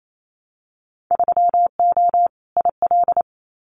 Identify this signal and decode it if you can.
Morse code, heard as a keyed tone, about 28 wpm, 700 Hz. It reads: 3O IL
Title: Winter Field Day Exchanges 28wpm